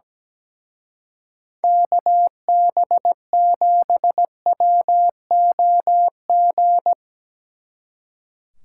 Morse code, heard as a keyed tone, about 18 wpm, 700 Hz. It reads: KB7WOG
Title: Speed Racing - USA 2x3 Call Signs 17wpm